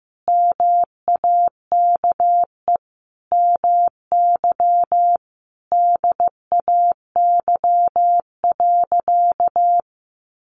Morse code, heard as a keyed tone, about 15 wpm, 700 Hz. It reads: MAKE MY DAY.
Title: Sentences from Top 1100 Words 15wpm